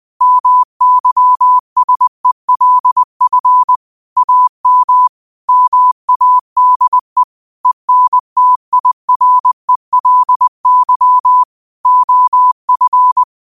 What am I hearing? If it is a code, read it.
Morse code, heard as a keyed tone, about 20 wpm, 1000 Hz. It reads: MYSELF AM MADE ENTIRELY OF
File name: qotdfiles / 20wpm